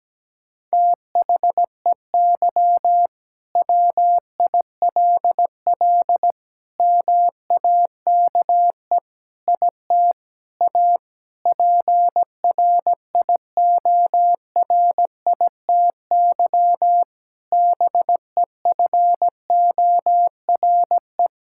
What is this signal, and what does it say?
Morse code, heard as a keyed tone, about 17 wpm, 700 Hz. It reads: THEY WILL MAKE IT A PRIORITY BEFORE
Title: Sentences from Top 1600 Words 17wpm